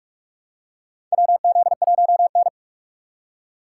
Morse code, heard as a keyed tone, about 44 wpm, 700 Hz. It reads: WZ1N
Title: USA Call Signs - 4 Characters - 4x Character Spacing - 45wpm